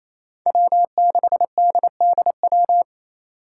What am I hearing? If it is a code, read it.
Morse code, heard as a keyed tone, about 28 wpm, 700 Hz. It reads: W6DDW
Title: Five Challenging Letter-Number Combinations 28wpm